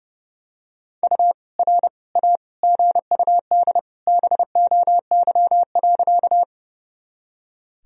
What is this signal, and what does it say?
Morse code, heard as a keyed tone, about 32 wpm, 700 Hz. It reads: U R A GUD BOY.
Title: Sentences from Top 200 Words and Abbreviations 30wpm